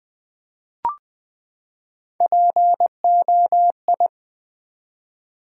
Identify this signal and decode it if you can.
Morse code, heard as a keyed tone, about 20 wpm, 700 Hz. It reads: POI
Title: Build A Word - 3 Letters At A Time - 9-Letter Words - 20wpm